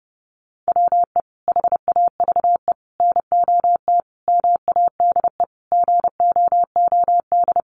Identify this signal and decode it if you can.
Morse code, heard as a keyed tone, about 30 wpm, 700 Hz. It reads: WE HAVE NOT MADE GOOD
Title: Sentences from Top 1400 Words 30wpm